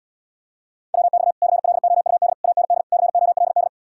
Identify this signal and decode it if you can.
Morse code, heard as a keyed tone, about 80 wpm, 700 Hz. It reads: WX CLOUD AND COLD